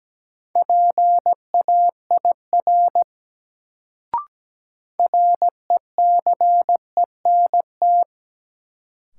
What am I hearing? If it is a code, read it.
Morse code, heard as a keyed tone, about 17 wpm, 700 Hz. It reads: PAIR RECENT
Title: Sets of 3 Words - New Word Drill - 17wpm